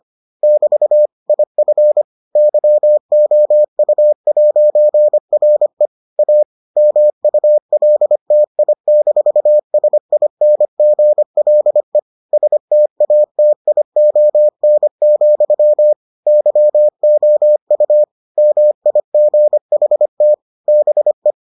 Morse code, heard as a keyed tone, about 25 wpm, 600 Hz. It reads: <BT> IF YOU'RE A MULTI-SINGLE STATION, YOU MIGHT BE